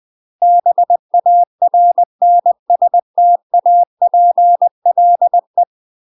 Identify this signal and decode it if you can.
Morse code, heard as a keyed tone, about 20 wpm, 700 Hz. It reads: BARNSTAPLE
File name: UK MASTER QSOs TRAINING / 20WPM / UK MASTER QSOs 0053